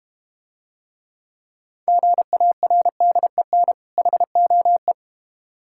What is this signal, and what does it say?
Morse code, heard as a keyed tone, about 30 wpm, 700 Hz. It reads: GARDEN HOE
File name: practice / 25wpm / Difficult Words - Speed-Racing with Hints - 25wpm